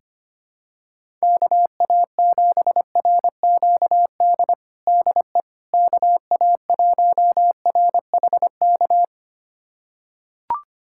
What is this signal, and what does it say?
Morse code, heard as a keyed tone, about 25 wpm, 700 Hz. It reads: KA7RQD DE KA1RHK
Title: USA Callsign DE Callsign - 6 Characters - 25wpm